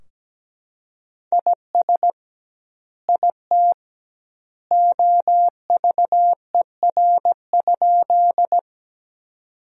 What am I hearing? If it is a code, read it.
Morse code, heard as a keyed tone, about 17 wpm, 700 Hz. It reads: IS IT OVER?
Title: 2x Word-Spacing - Sentences from Top 100 Words - 17wpm